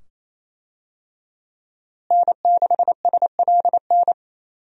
Morse code, heard as a keyed tone, about 27 wpm, 700 Hz. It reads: N6SLN
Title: USA Call Signs - 5 Characters - Repeated 2x - 28wpm